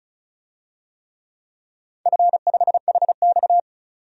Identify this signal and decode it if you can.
Morse code, heard as a keyed tone, about 36 wpm, 700 Hz. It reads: F5HX
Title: DXCC-100 Call Signs - 4 Characters - 35wpm